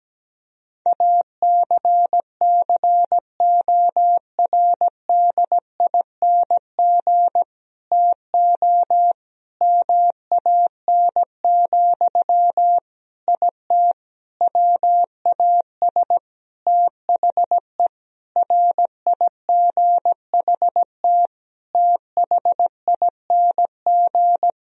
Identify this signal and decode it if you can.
Morse code, heard as a keyed tone, about 17 wpm, 700 Hz. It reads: ACCORDING TO MAN, IT WAS THE RIGHT THING
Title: Sentences from Top 1000 Words - Review Part 4 - 17wpm